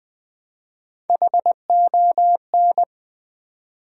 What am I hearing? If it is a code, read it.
Morse code, heard as a keyed tone, about 20 wpm, 700 Hz. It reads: HON
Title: Build A Word - 1 Letter At A Time - 5-Letter Words - 20wpm